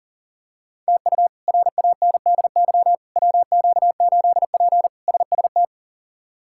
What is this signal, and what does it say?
Morse code, heard as a keyed tone, about 40 wpm, 700 Hz. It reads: TU RANDY WQ8P SST
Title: K1USN SST QSO Practice - 40wpm